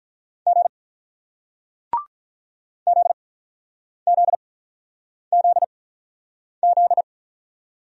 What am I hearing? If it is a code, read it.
Morse code, heard as a keyed tone, about 46 wpm, spaced as for 9 wpm, 700 Hz. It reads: G ZZZZ